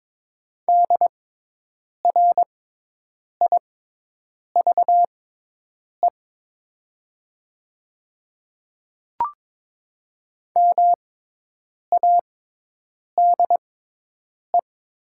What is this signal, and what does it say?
Morse code, heard as a keyed tone, about 22 wpm, spaced as for 8 wpm, 700 Hz. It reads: DRIVE MADE
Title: 6x Character Spacing - Top 500 Words - 22wpm